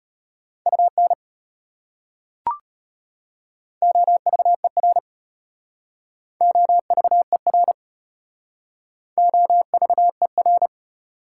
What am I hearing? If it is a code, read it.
Morse code, heard as a keyed tone, about 33 wpm, 700 Hz. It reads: UN OVER OVER OVER